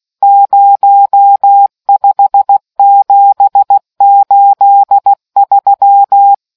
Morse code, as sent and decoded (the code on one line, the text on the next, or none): ----- ..... --... ---.. ...--
05783